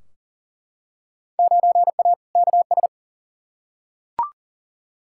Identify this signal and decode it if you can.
Morse code, heard as a keyed tone, about 40 wpm, 700 Hz. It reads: 9A KS
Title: Field Day Exchanges Encore - 40wpm